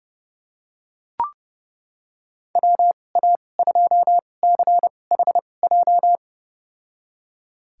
Morse code, heard as a keyed tone, about 31 wpm, spaced as for 22 wpm, 700 Hz. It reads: WA2CHJ
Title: USA Call Signs - 6 Characters - 2x Character Spacing - 30wpm